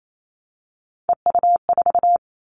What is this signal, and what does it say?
Morse code, heard as a keyed tone, about 28 wpm, 700 Hz. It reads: EU4
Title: Callsign Prefixes for Top 50 DXCC - 28wpm